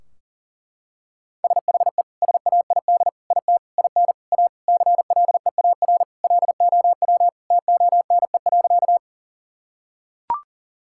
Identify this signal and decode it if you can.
Morse code, heard as a keyed tone, about 37 wpm, 700 Hz. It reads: SHE SAID IT IN A CLEAR LOW TONE.
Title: Sentences from Top 1600 Words 40wpm